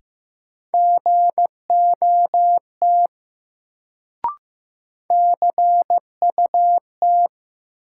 Morse code, heard as a keyed tone, about 15 wpm, 700 Hz. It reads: GOT CUT
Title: Sentences from Top 300 Words - New Word Drill - 15wpm